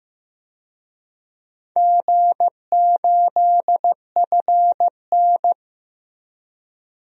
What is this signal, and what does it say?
Morse code, heard as a keyed tone, about 15 wpm, 700 Hz. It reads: G8FN
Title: DXCC-100 Call Signs - 4 Characters - Repeated 3x - 15wpm